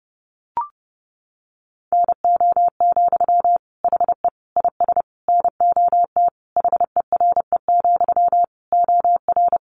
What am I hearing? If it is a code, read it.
Morse code, heard as a keyed tone, about 30 wpm, 700 Hz. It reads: NO, HE IS NOT HERE, OR
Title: Sentences from Top 1400 Words 30wpm